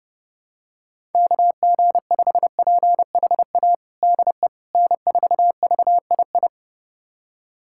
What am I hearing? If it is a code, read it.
Morse code, heard as a keyed tone, about 32 wpm, 700 Hz. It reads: KG5PHA DE N4VII
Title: USA Callsign DE Callsign 30wpm